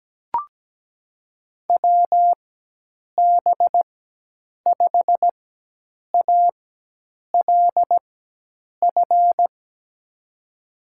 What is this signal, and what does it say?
Morse code, heard as a keyed tone, about 17 wpm, spaced as for 8 wpm, 700 Hz. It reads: WB5ALF
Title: USA Call Signs - 6 Characters - 4x Character Spacing - 17wpm